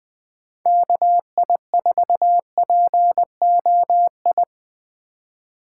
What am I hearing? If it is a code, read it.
Morse code, heard as a keyed tone, about 20 wpm, 700 Hz. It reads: KI4POI